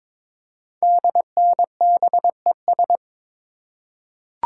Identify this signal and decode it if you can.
Morse code, heard as a keyed tone, about 22 wpm, 700 Hz. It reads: DNBES
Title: Code Groups - 5 Letters - 22wpm